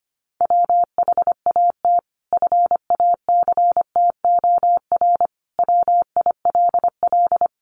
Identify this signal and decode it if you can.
Morse code, heard as a keyed tone, about 25 wpm, 700 Hz. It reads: WHAT FACTOR WILL
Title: Sentences from Top 1300 Words 25wpm